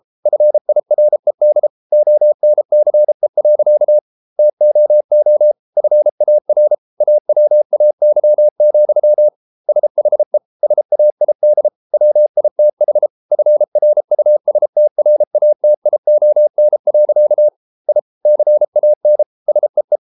Morse code, heard as a keyed tone, about 33 wpm, 600 Hz. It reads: FIRED ONCE. TOO FAR AWAY, SHE SAID WITH FRUSTRATION. I CAN SEE